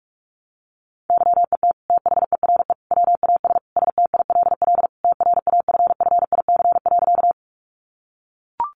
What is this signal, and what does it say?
Morse code, heard as a keyed tone, about 44 wpm, 700 Hz. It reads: YET THERE WAS STILL TRAFFIC.
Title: Sentences from Top 1300 Words 45wpm